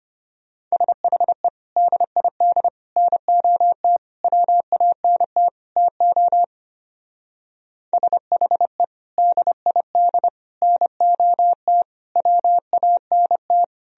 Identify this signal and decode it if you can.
Morse code, heard as a keyed tone, about 29 wpm, 700 Hz. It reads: SHE DID NOT WANT TO SHE DID NOT WANT
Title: Speed-Racing - Sets of 5 Words 25wpm